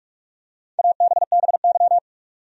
Morse code, heard as a keyed tone, about 46 wpm, 700 Hz. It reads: ABBY
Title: USA Common Names - 45wpm